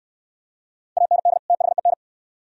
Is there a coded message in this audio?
Yes: Morse code from 1.0 to 1.9 s, heard as a keyed tone, about 69 wpm, 690 Hz. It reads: AND THEN